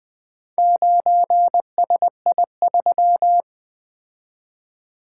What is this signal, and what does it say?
Morse code, heard as a keyed tone, about 20 wpm, 700 Hz. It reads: 9SI3
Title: Four Letter-Number Combinations - 20wpm